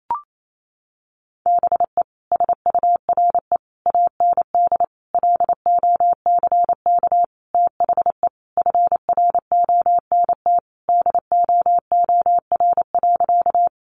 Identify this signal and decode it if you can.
Morse code, heard as a keyed tone, about 28 wpm, 700 Hz. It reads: BE SURE AND LOCK THE FRONT DOOR.
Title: Sentences from Top 1300 Words 28wpm